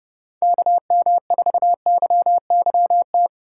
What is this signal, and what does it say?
Morse code, heard as a keyed tone, about 30 wpm, 700 Hz. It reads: KM4YYT
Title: Speed Racing - USA 2x3 Call Signs 25wpm